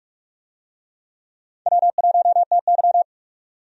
Morse code, heard as a keyed tone, about 46 wpm, 700 Hz. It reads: W1TY
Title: USA Call Signs - 4 Characters - 45wpm